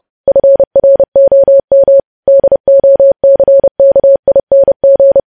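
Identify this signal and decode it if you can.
Morse code, heard as a keyed tone, about 30 wpm, 550 Hz. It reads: FROM DOCKING
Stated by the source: News headlines sent in Morse at 30 words per minute.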